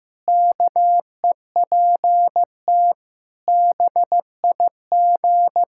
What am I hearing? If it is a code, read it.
Morse code, heard as a keyed tone, about 16 wpm, 700 Hz. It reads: KEPT BIG
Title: Pangram Challenge v2 - 15wpm